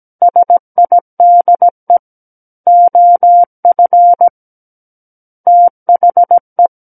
Morse code, heard as a keyed tone, about 17 wpm, 700 Hz. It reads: SIDE OF THE
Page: SKCC Learning Center - Morse Code Practice Files